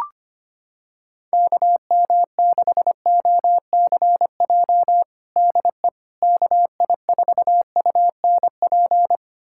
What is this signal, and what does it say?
Morse code, heard as a keyed tone, about 25 wpm, 700 Hz. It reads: KM6OCJ DE KI4UNP
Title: USA Callsign DE Callsign - 6 Characters - 25wpm